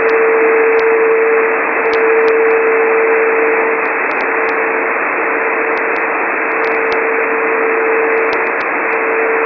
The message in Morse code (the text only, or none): none